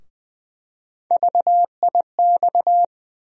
Morse code, heard as a keyed tone, about 20 wpm, 700 Hz. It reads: VIX
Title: USA Call Sign Postfixes - 3 Letters 20wpm